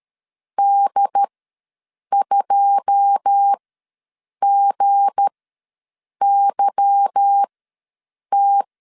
Morse code, heard as a keyed tone, about 13 wpm, spaced as for 7 wpm, 800 Hz. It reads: D2GYT
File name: LEZIONI / 9 286 ws 12wpm 3351